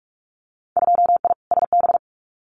Morse code, heard as a keyed tone, about 46 wpm, 700 Hz. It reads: 2I SB
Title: Winter Field Day Exchanges Encore 45wpm